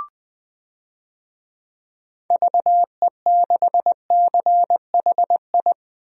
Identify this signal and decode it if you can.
Morse code, heard as a keyed tone, about 20 wpm, 700 Hz. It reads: VE6CHI
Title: DXCC-100 Call Signs - 6 Characters - Repeated 2x - 20wpm